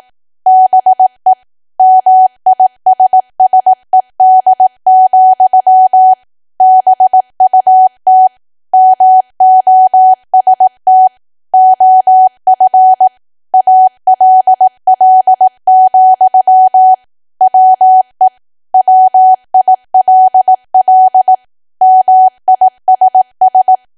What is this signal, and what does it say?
Morse code, heard as a keyed tone, about 18 wpm, 750 Hz. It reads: BE MISSED, BUT MOST OF ALL, WE WILL MISS
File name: Archive / 18 WPM / 18WPM